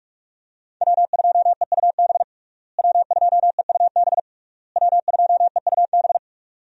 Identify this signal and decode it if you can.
Morse code, heard as a keyed tone, about 46 wpm, 700 Hz. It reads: W2EUB W2EUB W2EUB
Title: USA Call Signs - 5 Characters - Repeated 3x - 45wpm